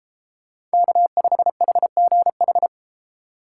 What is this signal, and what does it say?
Morse code, heard as a keyed tone, about 33 wpm, 700 Hz. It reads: K5HGH